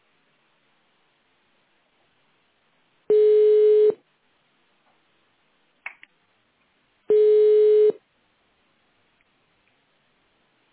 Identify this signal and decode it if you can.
Ringback tone (425 Hz, 1 s on, 3 s off).